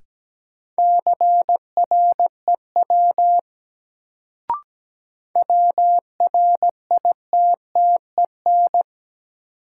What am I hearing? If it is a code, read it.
Morse code, heard as a keyed tone, about 17 wpm, 700 Hz. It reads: CREW WRITTEN
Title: Sentences from Top 1500 Words - New Word Drill - 17wpm